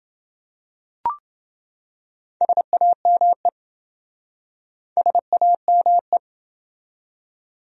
Morse code, heard as a keyed tone, about 27 wpm, 700 Hz. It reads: SAME SAME